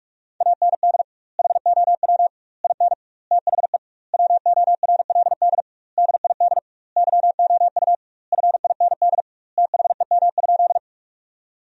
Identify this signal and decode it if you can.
Morse code, heard as a keyed tone, about 41 wpm, 700 Hz. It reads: AND HOW IN THE WORLD DID YOU FIND THEM?
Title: 2x Word-Spacing - Sentences from Top 200 Words - 45wpm